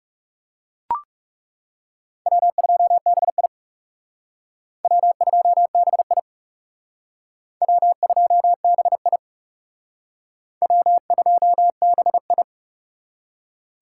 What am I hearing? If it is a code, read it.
Morse code, heard as a keyed tone, about 37 wpm, 700 Hz. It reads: W2BI W2BI W2BI W2BI
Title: Speed Racing - USA 1x2 Call Signs 30wpm